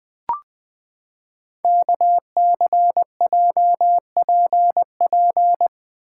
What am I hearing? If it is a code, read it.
Morse code, heard as a keyed tone, about 20 wpm, 700 Hz. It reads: KCJPP
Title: Code Groups - 5 Letters - 20wpm